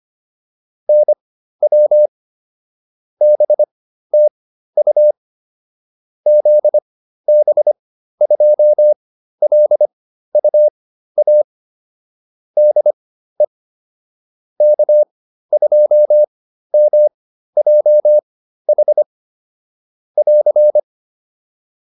Morse code, heard as a keyed tone, about 25 wpm, spaced as for 13 wpm, 600 Hz. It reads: NW BTU ZB2LUA DE K2MJH <AR>